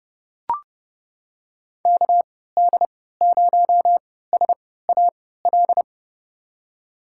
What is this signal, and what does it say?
Morse code, heard as a keyed tone, about 30 wpm, spaced as for 17 wpm, 700 Hz. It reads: KD0SAL
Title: USA Call Signs - 6 Characters - 3x Character Spacing - 30wpm